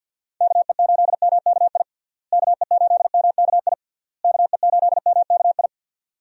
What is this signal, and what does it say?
Morse code, heard as a keyed tone, about 50 wpm, 700 Hz. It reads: KE8MKI KE8MKI KE8MKI
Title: USA Call Signs - 6 Characters - Repeated 3x - 50wpm